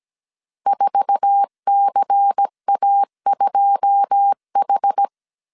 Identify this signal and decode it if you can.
Morse code, heard as a keyed tone, about 17 wpm, 800 Hz. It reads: 4CA2H
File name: CW / DET / 1 9 1500C S C 16wpm 2909